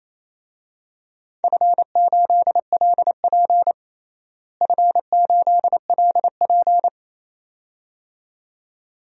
Morse code, heard as a keyed tone, about 28 wpm, 700 Hz. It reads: F8LP F8LP
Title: DXCC-100 Call Signs - 4 Characters - Repeated 2x - 28wpm